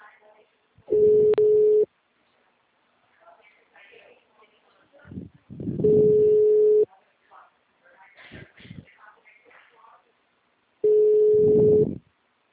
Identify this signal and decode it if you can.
Ringback tone (Europe, 425 Hz), with speech.